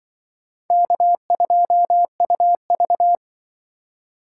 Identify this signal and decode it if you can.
Morse code, heard as a keyed tone, about 24 wpm, 700 Hz. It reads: K2UV
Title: Speed Racing - USA Call Signs 20wpm